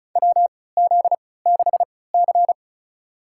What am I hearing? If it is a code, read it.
Morse code, heard as a keyed tone, about 36 wpm, spaced as for 20 wpm, 700 Hz. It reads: WZ6C
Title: USA Call Signs - 3x Character Spacing - 35wpm